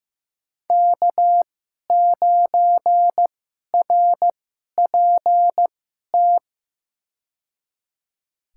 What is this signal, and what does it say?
Morse code, heard as a keyed tone, about 15 wpm, spaced as for 11 wpm, 700 Hz. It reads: K9RPT
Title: USA Call Signs - 2x Character Spacing - 15wpm